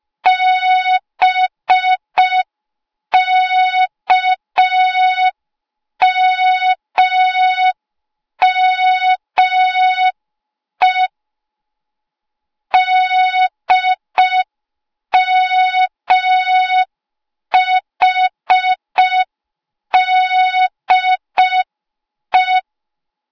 Morse code, heard as a keyed tone, about 5 wpm, 3700 Hz. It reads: BKMME DMHDE
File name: Morze-HA2MN / hang / 08KXC HA2MN